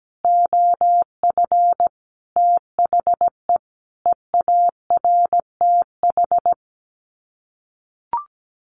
Morse code, heard as a keyed tone, about 17 wpm, 700 Hz. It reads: OF THE EARTH
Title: Sets of 5 Words Encore 17wpm